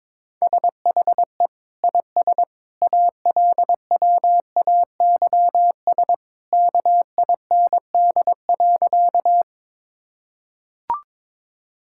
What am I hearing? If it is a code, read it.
Morse code, heard as a keyed tone, about 22 wpm, 700 Hz. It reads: SHE IS ALWAYS KIND.
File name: practice / 22wpm / Sentences with 2-Word Phrases - 22wpm